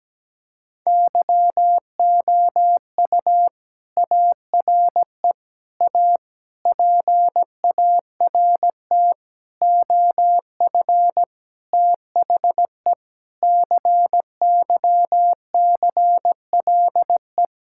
Morse code, heard as a keyed tone, about 17 wpm, 700 Hz. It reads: YOU ARE A PART OF THE CYCLE
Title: Sentences from Top 1000 Words - Review Part 6 - 17wpm